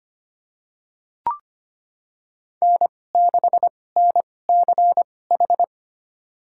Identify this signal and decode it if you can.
Morse code, heard as a keyed tone, about 25 wpm, spaced as for 18 wpm, 700 Hz. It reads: N6NCH
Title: USA Call Signs - 5 Characters - 2x Character Spacing - 25wpm